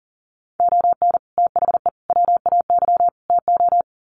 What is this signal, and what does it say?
Morse code, heard as a keyed tone, about 40 wpm, 700 Hz. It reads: ON THE WAY TO
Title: Sets of 4 Words 40wpm